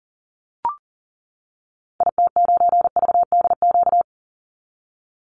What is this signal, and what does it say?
Morse code, heard as a keyed tone, about 40 wpm, 700 Hz. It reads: IT9VDQ
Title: Call Signs - 40wpm